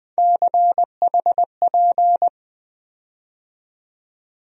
Morse code, heard as a keyed tone, about 20 wpm, 700 Hz. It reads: CHP
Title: USA Call Sign Postfixes - 3 Letters 20wpm